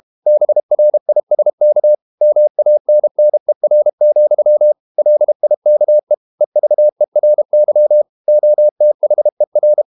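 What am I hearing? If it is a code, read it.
Morse code, heard as a keyed tone, about 32 wpm, 600 Hz. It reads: BRISK MANNER, LIKE EVERY OTHER
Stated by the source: Morse code practice at 32 wpm from CW med Gnister, reading Ken Follett's novel Never in English.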